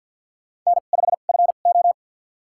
Morse code, heard as a keyed tone, about 52 wpm, spaced as for 36 wpm, 700 Hz. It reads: N5FO
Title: USA Call Signs - 4 Characters - 2x Character Spacing - 50wpm